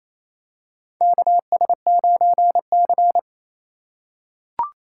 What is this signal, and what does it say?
Morse code, heard as a keyed tone, about 28 wpm, 700 Hz. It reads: KS9C
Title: USA Call Signs - 4 Characters - 4x Character Spacing - 28wpm